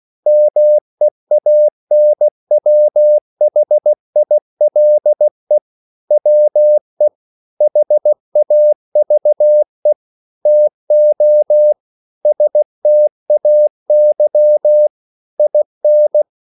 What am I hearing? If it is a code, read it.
Morse code, heard as a keyed tone, about 16 wpm, 600 Hz. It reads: MEANWHILE WE HAVE TO STAY IN